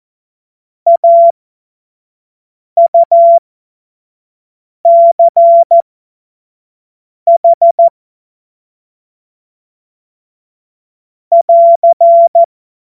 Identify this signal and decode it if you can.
Morse code, heard as a keyed tone, about 14 wpm, spaced as for 5 wpm, 680 Hz. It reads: AUCH <AR>
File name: Morsekurs / Witze / Witz 10. August 2021, 25 F70